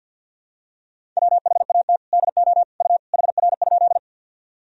Morse code, heard as a keyed tone, about 49 wpm, 700 Hz. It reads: WHAT DO U HR?
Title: Sentences from Top 300 Words and Abbreviations 50wpm